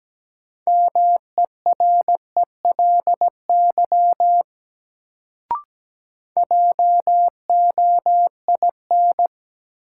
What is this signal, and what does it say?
Morse code, heard as a keyed tone, about 17 wpm, 700 Hz. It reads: MERELY JOIN
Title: Sentences from Top 1400 Words - New Word Drill Alt - 17wpm